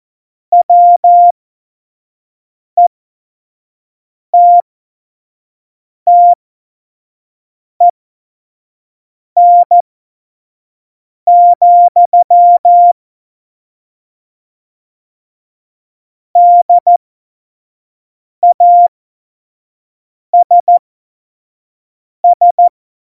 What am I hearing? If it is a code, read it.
Morse code, heard as a keyed tone, about 14 wpm, spaced as for 5 wpm, 700 Hz. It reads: WETTEN, DASS
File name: Morsekurs / Witze / Witz 04. Juni 2019, 25 F70